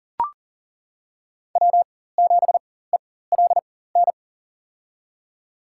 Morse code, heard as a keyed tone, about 39 wpm, spaced as for 19 wpm, 700 Hz. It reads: W7ELN
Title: USA Call Signs - 5 Characters - 4x Character Spacing - 40wpm